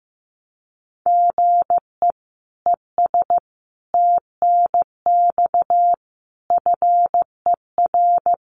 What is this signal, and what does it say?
Morse code, heard as a keyed tone, about 15 wpm, 700 Hz. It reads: GE ES TNX FER
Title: Common QSO Phrases Encore 15wpm